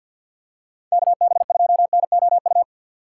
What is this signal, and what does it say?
Morse code, heard as a keyed tone, about 52 wpm, 700 Hz. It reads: KB2NOU
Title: USA Call Signs - 6 Characters - 6x Character Spacing - 50wpm